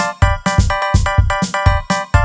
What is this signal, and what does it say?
There is no code.